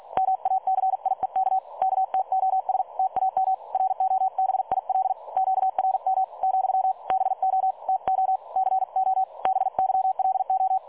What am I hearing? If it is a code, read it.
Morse code, heard as a keyed tone, about 48 wpm, 760 Hz. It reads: MACIEK KTOS TAM ROBIL QRM , BO TO CO BYLO
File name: dlazaawansowanych / nagraniazpasma / websdr recording 3565.0kHz